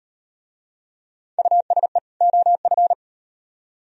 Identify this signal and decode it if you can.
Morse code, heard as a keyed tone, about 38 wpm, 700 Hz. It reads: USE OF